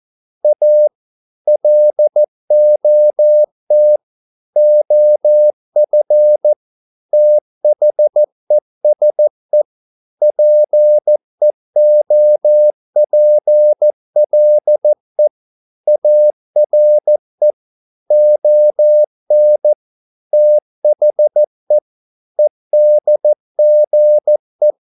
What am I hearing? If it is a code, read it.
Morse code, heard as a keyed tone, about 14 wpm, 600 Hz. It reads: A LOT OF THESE PEOPLE ARE ON THE EDGE